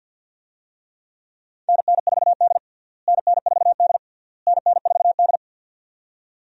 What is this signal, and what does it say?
Morse code, heard as a keyed tone, about 52 wpm, 700 Hz. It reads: NN4D NN4D NN4D
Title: USA Call Signs - 4 Characters - Repeated 3x - 50wpm